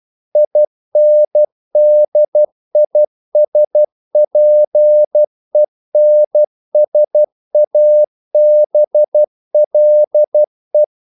Morse code, heard as a keyed tone, about 12 wpm, 600 Hz. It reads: INDISPENSABLE
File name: CW / Practice / EnglishQuotations0312